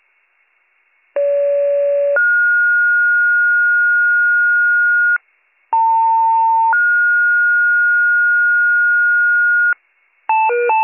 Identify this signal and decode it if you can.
Two-tone page, heard as a one-second tone, then a three-second tone, about 570 then 1430 Hz, about 900 then 1430 Hz. It is heard from 1.2 to 9.7 s.